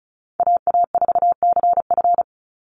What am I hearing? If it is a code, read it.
Morse code, heard as a keyed tone, about 34 wpm, 700 Hz. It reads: AA4CF
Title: Call Signs Repeated 2x - 35wpm